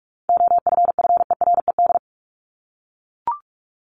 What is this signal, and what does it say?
Morse code, heard as a keyed tone, about 44 wpm, 700 Hz. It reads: OFFERED
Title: ED Suffix - Top 100 Words with Base Words 45wpm